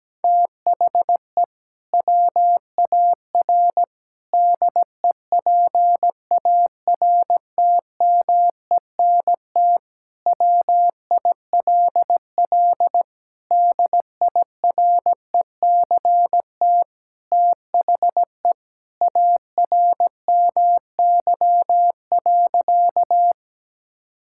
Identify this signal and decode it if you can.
Morse code, heard as a keyed tone, about 17 wpm, 700 Hz. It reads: THE WAR DEPARTMENT WILL DIRECT THE ARMY.
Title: Sentences from Top 1000 Words - Review Part 5 - 17wpm